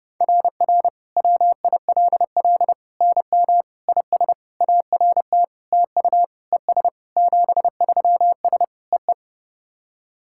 Morse code, heard as a keyed tone, about 32 wpm, 690 Hz. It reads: RR WILL NM IS ART TU ES 73S EE
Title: SOTA QSO Practice 30wpm